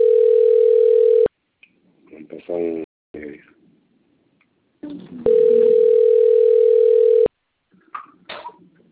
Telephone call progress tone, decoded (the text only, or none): ringback tone